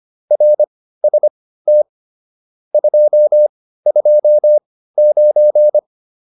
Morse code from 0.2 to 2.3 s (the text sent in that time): RST